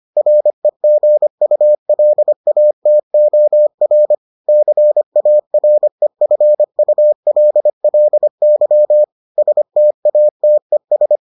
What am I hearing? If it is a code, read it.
Morse code, heard as a keyed tone, about 25 wpm, 600 Hz. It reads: REGULATOR CAREFULLY STATES